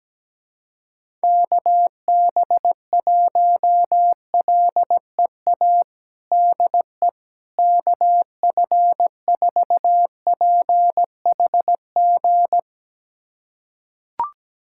Morse code, heard as a keyed tone, about 17 wpm, 700 Hz. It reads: KB1LEA DE KF4PHG
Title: USA Callsign DE Callsign 17wpm